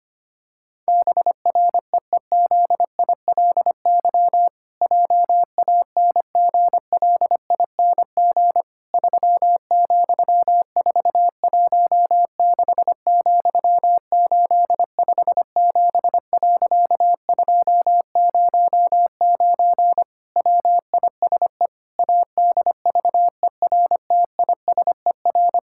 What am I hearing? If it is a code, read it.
Morse code, heard as a keyed tone, about 25 wpm, 700 Hz. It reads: BREEZILY JANGLING 3,416,857.209 WISE ADVERTISER
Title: Pangram Challenge v2 - 25wpm